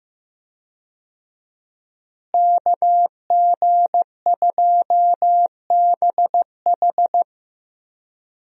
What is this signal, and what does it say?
Morse code, heard as a keyed tone, about 15 wpm, 700 Hz. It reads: KG2BH
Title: USA Call Signs - 5 Characters - Repeated 3x - 15wpm